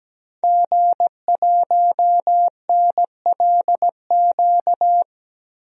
Morse code, heard as a keyed tone, about 17 wpm, 700 Hz. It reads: G1NLQ
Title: DXCC-100 Call Signs - 5 Characters - 17wpm